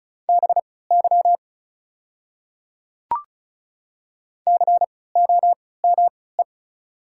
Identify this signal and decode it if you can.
Morse code, heard as a keyed tone, about 36 wpm, spaced as for 20 wpm, 700 Hz. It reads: BY COME
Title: 3x Character Spacing - Top 200 Words - 35wpm